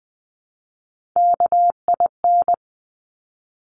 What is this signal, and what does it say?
Morse code, heard as a keyed tone, about 20 wpm, 700 Hz. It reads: KIN
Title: Top Three Letter Combinations in Words 20wpm